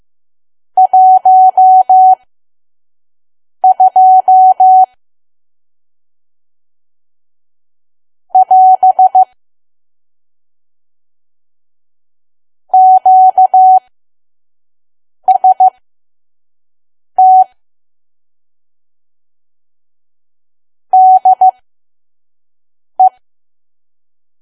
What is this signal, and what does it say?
Morse code, heard as a keyed tone, about 15 wpm, spaced as for 5 wpm, 750 Hz. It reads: 12 <AS> QST DE
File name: Archive / 5 WPM files / PM